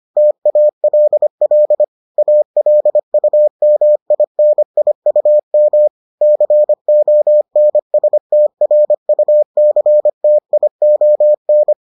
Morse code, heard as a keyed tone, about 25 wpm, 600 Hz. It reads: TALL ALUMINIUM CONSTRUCTION